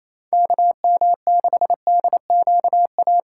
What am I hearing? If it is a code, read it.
Morse code, heard as a keyed tone, about 28 wpm, 700 Hz. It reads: KM6DQA
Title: USA Callsign DE Callsign 28wpm